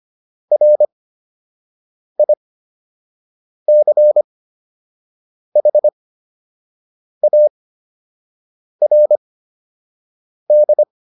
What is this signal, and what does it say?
Morse code, heard as a keyed tone, about 25 wpm, spaced as for 6 wpm, 600 Hz. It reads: RICHARD
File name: QSO 203 06